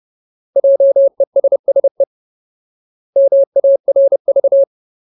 Morse code, heard as a keyed tone, about 30 wpm, 550 Hz. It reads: JESSE MARV
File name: Ultimate Word List / 30WPM Ultimate